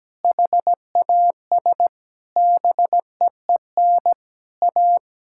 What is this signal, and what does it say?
Morse code, heard as a keyed tone, about 17 wpm, 700 Hz. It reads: HAS BEEN A
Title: Sentences from Top 1000 Words - Review Part 6 - 17wpm